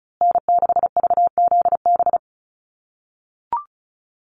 Morse code, heard as a keyed tone, about 34 wpm, 700 Hz. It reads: N6VZB
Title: USA Call Signs Repeated 3x 35wpm